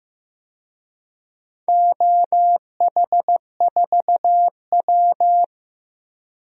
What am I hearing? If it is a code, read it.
Morse code, heard as a keyed tone, about 15 wpm, 700 Hz. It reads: OH4W
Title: DXCC-100 Call Signs - 4 Characters - 15wpm